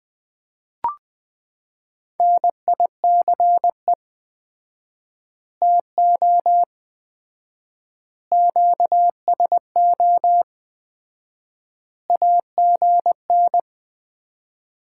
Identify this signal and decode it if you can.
Morse code, heard as a keyed tone, about 21 wpm, 700 Hz. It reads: NICE TO QSO AGN
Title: Common QSO Phrases - 4x Word-Spacing - 20wpm